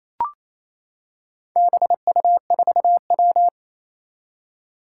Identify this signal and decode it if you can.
Morse code, heard as a keyed tone, about 27 wpm, 700 Hz. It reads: BU4W
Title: Four Challenging Letter-Number Combinations 28wpm